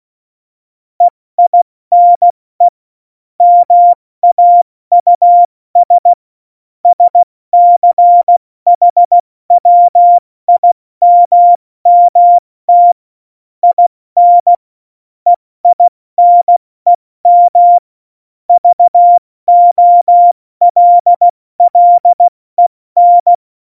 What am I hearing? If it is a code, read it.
Morse code, heard as a keyed tone, about 16 wpm, 700 Hz. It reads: EINE MAUS SCHWIMMT IN EINEM VOLLEN
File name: Morsekurs / Witze / Witz 18. September 2018, 70 F80